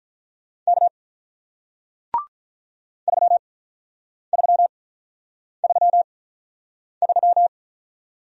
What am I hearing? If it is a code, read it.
Morse code, heard as a keyed tone, about 47 wpm, spaced as for 9 wpm, 700 Hz. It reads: K 3333